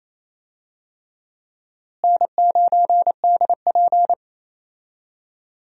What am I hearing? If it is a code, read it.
Morse code, heard as a keyed tone, about 28 wpm, 700 Hz. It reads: N9DP
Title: USA Call Signs - 4 Characters - Repeated 3x - 28wpm